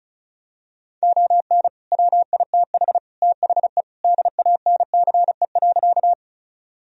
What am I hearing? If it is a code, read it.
Morse code, heard as a keyed tone, about 34 wpm, 700 Hz. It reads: ON WITH THE DANCE.
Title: Sentences from Top 1000 Words - Review Part 4 - 35wpm